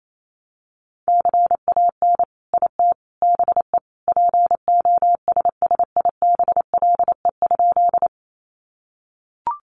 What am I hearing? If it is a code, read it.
Morse code, heard as a keyed tone, about 28 wpm, 700 Hz. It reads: CAN IT BE POSSIBLE?
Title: Sentences from Top 500 Words 28wpm